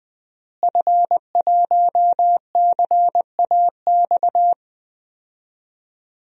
Morse code, heard as a keyed tone, about 20 wpm, 700 Hz. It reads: F1CAX
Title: DXCC-100 Call Signs - 5 Characters - Repeated 2x - 20wpm